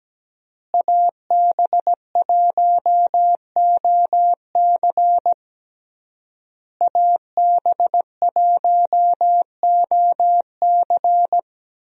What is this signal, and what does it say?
Morse code, heard as a keyed tone, about 17 wpm, 700 Hz. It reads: AB1OC AB1OC
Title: USA Call Signs - 5 Characters - Repeated 3x - 17wpm